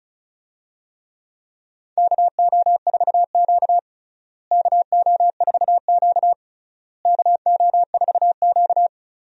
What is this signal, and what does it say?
Morse code, heard as a keyed tone, about 36 wpm, 700 Hz. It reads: KO4Q KO4Q KO4Q
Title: USA Call Signs - 4 Characters - Repeated 3x - 35wpm